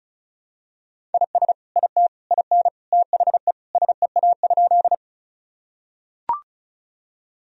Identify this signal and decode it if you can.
Morse code, heard as a keyed tone, about 36 wpm, 700 Hz. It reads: IS IT IN THE SEA?
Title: Sentences from Top 1000 Words - Review Part 1 - 35wpm